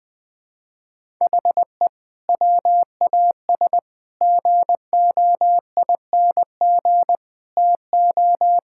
Morse code, heard as a keyed tone, about 20 wpm, 700 Hz. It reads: HE WAS GOING TO
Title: Sentences with 4-Word Phrases - Spoken Beginning - 20wpm